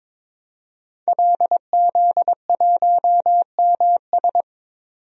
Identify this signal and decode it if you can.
Morse code, heard as a keyed tone, about 22 wpm, 700 Hz. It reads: LZ1MS
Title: DXCC-100 Call Signs - 5 Characters - 22wpm